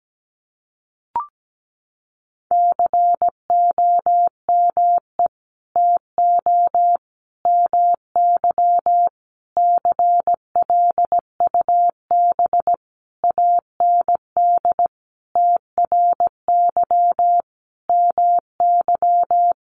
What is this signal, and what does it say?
Morse code, heard as a keyed tone, about 17 wpm, 700 Hz. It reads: COME TO MY CLUB AND TRY MY
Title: Sentences from Top 1500 Words 17wpm